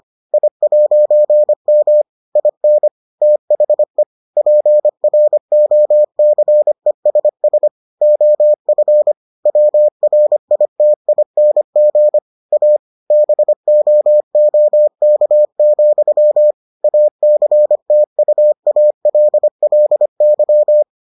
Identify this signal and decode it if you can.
Morse code, heard as a keyed tone, about 25 wpm, 600 Hz. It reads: I'M IN THE PROCESS OF WRITING A BOOK, ACTUALLY